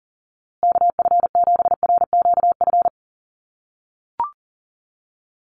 Morse code, heard as a keyed tone, about 40 wpm, 700 Hz. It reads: KF7RQF
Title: USA Call Signs Repeated 2x 40wpm